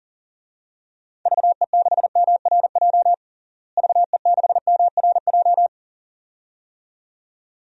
Morse code, heard as a keyed tone, about 40 wpm, 700 Hz. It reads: VE6MRJ VE6MRJ
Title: DXCC-100 Call Signs - 6 Characters - Repeated 2x - 40wpm